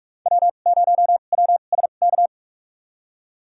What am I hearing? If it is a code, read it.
Morse code, heard as a keyed tone, about 44 wpm, spaced as for 32 wpm, 700 Hz. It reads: W0WSK
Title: USA Call Signs - 5 Characters - 2x Character Spacing - 45wpm